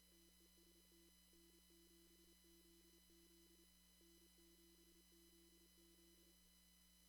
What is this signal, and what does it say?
Morse code, heard as a keyed tone, about 27 wpm, 340 Hz, with noise. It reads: TEST N7ZZ N7ZZ
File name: CW / kt5j wpx cw-op-k5pi